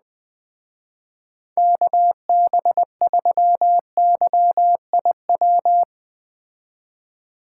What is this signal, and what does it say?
Morse code, heard as a keyed tone, about 20 wpm, 700 Hz. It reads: KB3YIW